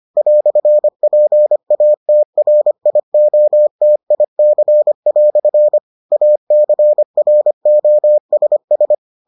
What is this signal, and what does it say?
Morse code, heard as a keyed tone, about 25 wpm, 600 Hz. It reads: "PATRIOTIC" ACROSS